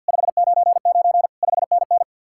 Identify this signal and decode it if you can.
Morse code, heard as a keyed tone, about 67 wpm, spaced as for 58 wpm, 700 Hz. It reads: 599 5NN
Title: Speed-Racing Common QSO Phrases Encore - 50wpm